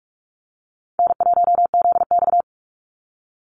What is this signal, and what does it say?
Morse code, heard as a keyed tone, about 44 wpm, 700 Hz. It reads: N1ZX
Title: Easy Call Signs Repeated 3x - 45wpm